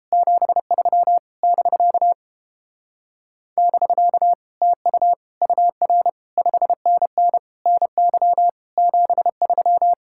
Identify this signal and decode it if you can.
Morse code, heard as a keyed tone, about 29 wpm, 700 Hz. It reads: 73 <BK> <BK> TU UR 5NN NY 73